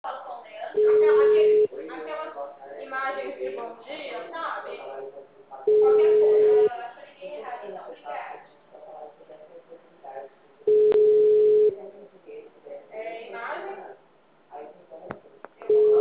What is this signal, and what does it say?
Ringback tone (Europe, 425 Hz), with speech.
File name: out- -2003- - - .9680